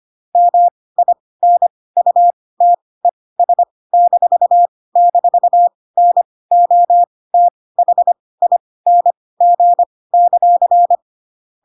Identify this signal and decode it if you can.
Morse code, heard as a keyed tone, about 25 wpm, spaced as for 18 wpm, 700 Hz. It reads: MINUTES--NOTHING;
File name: Pinocchio / Pinocchio ch1 12 25 ik1hge